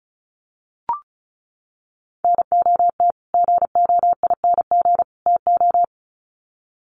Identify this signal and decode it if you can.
Morse code, heard as a keyed tone, about 35 wpm, 700 Hz. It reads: NOT GOING TO
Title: Sets of 3 Words 35wpm